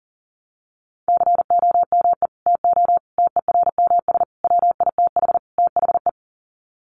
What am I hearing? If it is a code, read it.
Morse code, heard as a keyed tone, about 40 wpm, 700 Hz. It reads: COME TO TERMS WITH THE
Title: Sets of 5 Words Encore2 40wpm